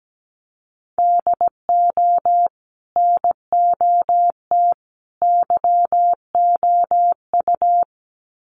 Morse code, heard as a keyed tone, about 17 wpm, 700 Hz. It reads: DO NOT YOU
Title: Sets of 3 Words 17wpm